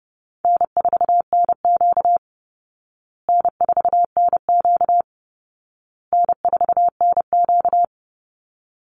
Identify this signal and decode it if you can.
Morse code, heard as a keyed tone, about 30 wpm, 700 Hz. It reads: N4NQ N4NQ N4NQ
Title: Call Signs Repeated 2x - 30wpm